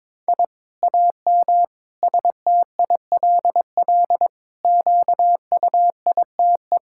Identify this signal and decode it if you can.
Morse code, heard as a keyed tone, about 21 wpm, 700 Hz. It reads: I AM STILL QUITE
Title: Sentences from Top 1000 Words - Review Part 5 - 22wpm